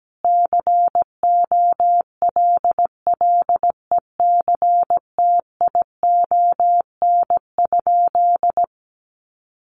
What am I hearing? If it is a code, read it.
Morse code, heard as a keyed tone, about 17 wpm, 700 Hz. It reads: COLLECTION?
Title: Sentences from Top 1200 Words 17wpm